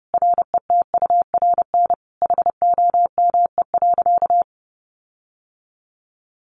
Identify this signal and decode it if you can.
Morse code, heard as a keyed tone, about 30 wpm, 700 Hz. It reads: RETURN HOME.
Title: Sentences from Top 1400 Words 30wpm